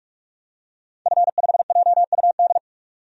Morse code, heard as a keyed tone, about 43 wpm, 700 Hz. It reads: F5JUD
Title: DXCC-100 Call Signs - 5 Characters - 45wpm